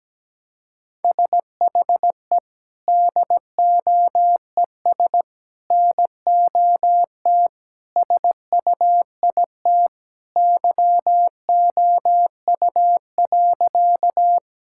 Morse code, heard as a keyed tone, about 17 wpm, 700 Hz. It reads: SHE DOES NOT SUIT YOU.
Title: Sentences from Top 1600 Words 17wpm